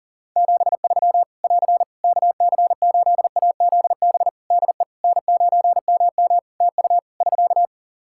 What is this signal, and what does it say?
Morse code, heard as a keyed tone, about 40 wpm, 700 Hz. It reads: 73 <AR> KC8AZB DE N9MM TU <SK>
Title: K1USN SST QSO Practice 40wpm